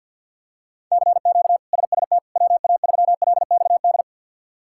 Morse code, heard as a keyed tone, about 48 wpm, 700 Hz. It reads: CQ SST WA3LXD
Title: K1USN SST QSO Practice - 50wpm